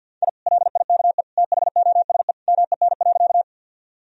Morse code, heard as a keyed tone, about 52 wpm, 700 Hz. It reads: I LIKE THOSE MEN.
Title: Sentences from Top 1000 Words - Review Part 1 - 50wpm